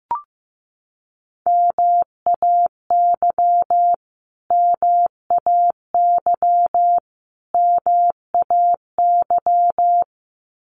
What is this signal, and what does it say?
Morse code, heard as a keyed tone, about 15 wpm, 700 Hz. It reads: MAY MAY MAY
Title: Top 300 Words Repeated 15wpm